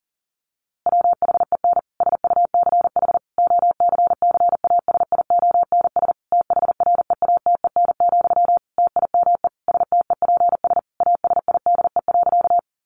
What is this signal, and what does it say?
Morse code, heard as a keyed tone, about 40 wpm, 700 Hz. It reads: WHEN SUCH OCCASIONS THREATEN, TIME STEPS ASIDE.
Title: A Sound Of Thunder 40wpm